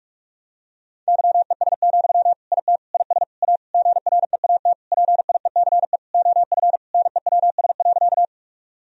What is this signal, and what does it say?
Morse code, heard as a keyed tone, about 46 wpm, 700 Hz. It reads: YES, IT IS A GREAT PIECE OF NEWS.
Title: Sentences from Top 1000 Words - Review Part 2 - 45wpm